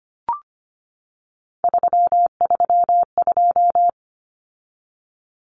Morse code, heard as a keyed tone, about 25 wpm, 700 Hz. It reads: 332
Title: Common Cut Numbers - 3-Digit 25wpm